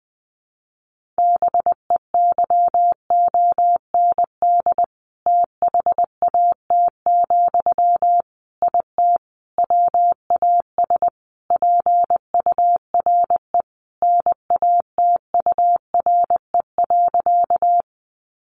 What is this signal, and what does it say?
Morse code, heard as a keyed tone, about 20 wpm, 700 Hz. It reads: BEYOND THAT, IT WAS PURE NATURE.
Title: Sentences from Top 1500 Words 20wpm